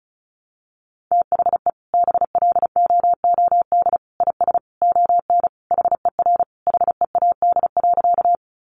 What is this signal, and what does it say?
Morse code, heard as a keyed tone, about 34 wpm, 700 Hz. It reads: THE BLOOD IS ON HER HEAD.
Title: Sentences from Top 1100 Words 35wpm